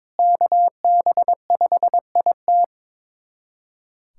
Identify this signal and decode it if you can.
Morse code, heard as a keyed tone, about 21 wpm, 700 Hz. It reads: KB5IT
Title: USA Callsign DE Callsign - 5 Characters - 22wpm